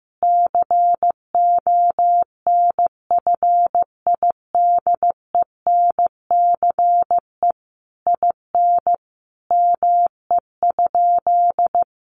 Morse code, heard as a keyed tone, about 15 wpm, 700 Hz. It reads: CONFIDENCE IN ME?
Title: Sentences from Top 1500 Words 15wpm